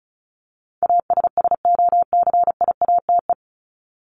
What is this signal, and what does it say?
Morse code, heard as a keyed tone, about 34 wpm, 700 Hz. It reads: ASSOCIATE
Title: ED Suffix - Top 100 Words with Base Words 35wpm